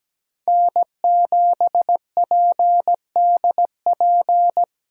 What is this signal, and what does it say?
Morse code, heard as a keyed tone, about 17 wpm, 700 Hz. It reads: N7PDP
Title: USA Call Signs - 5 Characters - 6x Character Spacing - 17wpm